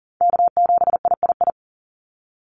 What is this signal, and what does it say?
Morse code, heard as a keyed tone, about 40 wpm, 700 Hz. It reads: K7III
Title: NA SOTA Chaser Callsigns 40wpm